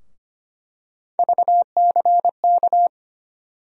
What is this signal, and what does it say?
Morse code, heard as a keyed tone, about 25 wpm, 700 Hz. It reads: VCK
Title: USA Call Sign Postfixes - 3 Letters 25wpm